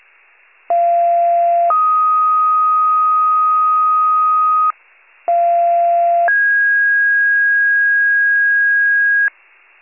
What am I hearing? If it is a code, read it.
Two-tone page, heard as a one-second tone, then a three-second tone, about 690 then 1250 Hz, about 690 then 1650 Hz, with noise. It is heard from 0.7 to 9.3 s.
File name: wcalert